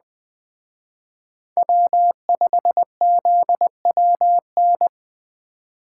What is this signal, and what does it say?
Morse code, heard as a keyed tone, about 21 wpm, 700 Hz. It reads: W5ZWN